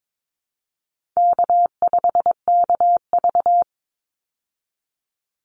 Morse code, heard as a keyed tone, about 22 wpm, 700 Hz. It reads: K5KV
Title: Easy Call Signs Repeated 3x - 22wpm